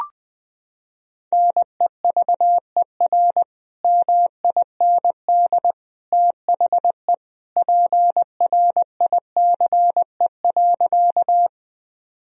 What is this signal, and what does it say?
Morse code, heard as a keyed tone, about 20 wpm, 700 Hz. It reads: NEVER MIND THE PRICE.
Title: Sentences from Top 1000 Words - Review Part 4 - 20wpm